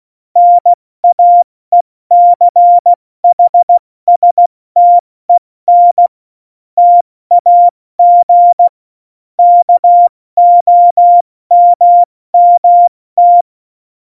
Morse code, heard as a keyed tone, about 16 wpm, 700 Hz. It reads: NAECHSTEN TAG KOMMT
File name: Morsekurs / Witze / Witz 14 April 2020, 70 F80